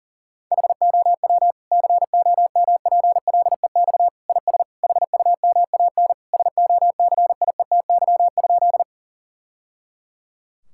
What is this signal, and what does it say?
Morse code, heard as a keyed tone, about 40 wpm, 700 Hz. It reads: HOW COMPLEX IS HUMAN SOCIETY?
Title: Sentences from Top 1000 Words - Review Part 6 - 40wpm